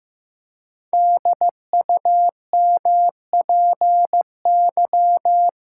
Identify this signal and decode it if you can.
Morse code, heard as a keyed tone, about 15 wpm, 700 Hz. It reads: DUMPY
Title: Pangram Challenge v2 - 15wpm